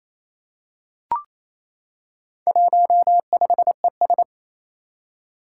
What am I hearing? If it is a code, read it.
Morse code, heard as a keyed tone, about 28 wpm, 700 Hz. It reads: 15ES
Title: Four Challenging Letter-Number Combinations 28wpm